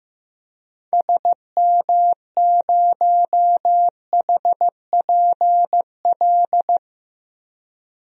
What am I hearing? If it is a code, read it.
Morse code, heard as a keyed tone, about 15 wpm, 700 Hz. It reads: SM0HPL
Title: DXCC-100 Call Signs - 6 Characters - 15wpm